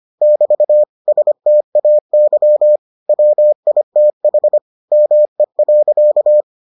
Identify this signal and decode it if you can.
Morse code, heard as a keyed tone, about 25 wpm, 600 Hz. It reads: <BT> STAY WITH ME.